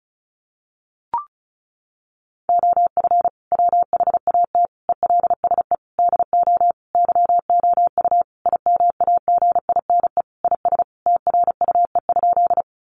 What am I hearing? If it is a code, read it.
Morse code, heard as a keyed tone, about 34 wpm, 700 Hz. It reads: OF WHAT ELSE DO YOU IMAGINE IS TRUE?
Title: Sentences from Top 1400 Words 35wpm